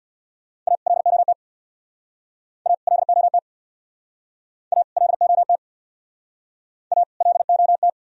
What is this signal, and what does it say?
Morse code, heard as a keyed tone, about 61 wpm, 700 Hz. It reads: A LOT A LOT A LOT A LOT
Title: Speed-Racing - Sets of 2 Words 50wpm